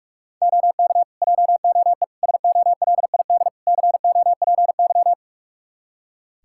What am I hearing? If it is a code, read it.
Morse code, heard as a keyed tone, about 42 wpm, 700 Hz. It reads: OK JOE SOLID COPY
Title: K1USN SST QSO Practice 45wpm